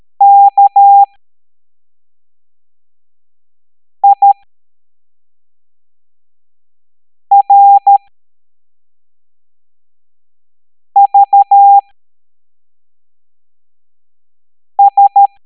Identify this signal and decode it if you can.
Morse code, heard as a keyed tone, about 13 wpm, spaced as for 3 wpm, 800 Hz. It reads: KIRVS
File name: hamdata / L05-EX01-SVY